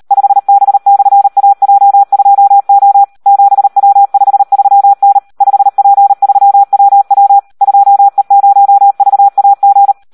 Morse code, heard as a keyed tone, about 38 wpm, 800 Hz. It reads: 5B/AJ2O 7W53N 5P3WW 2E0VAG
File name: Audio practica CW / SD 1 38